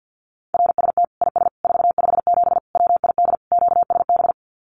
Morse code, heard as a keyed tone, about 49 wpm, 700 Hz. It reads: RST IS 457 WID QSB
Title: Common QSO Phrases 50wpm